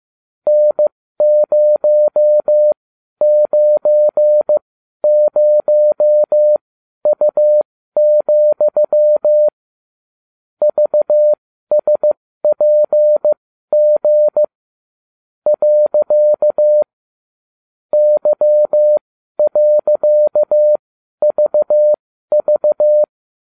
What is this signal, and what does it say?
Morse code, heard as a keyed tone, about 15 wpm, spaced as for 12 wpm, 600 Hz. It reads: N090U, VSPG . Y.VV